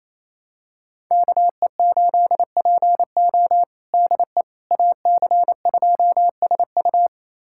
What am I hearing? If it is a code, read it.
Morse code, heard as a keyed tone, about 28 wpm, 700 Hz. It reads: KE8PO DE AC2SU
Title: USA Callsign DE Callsign - 5 Characters - 28wpm